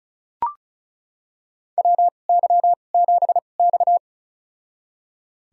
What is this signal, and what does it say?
Morse code, heard as a keyed tone, about 36 wpm, spaced as for 25 wpm, 700 Hz. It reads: WY7X
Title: USA Call Signs - 4 Characters - 2x Character Spacing - 35wpm